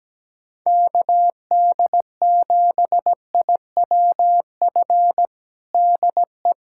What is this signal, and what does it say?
Morse code, heard as a keyed tone, about 17 wpm, 700 Hz. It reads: KD7IWF DE
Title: USA Callsign DE Callsign - 6 Characters - 17wpm